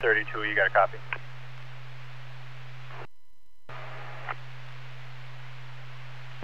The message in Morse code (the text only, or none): none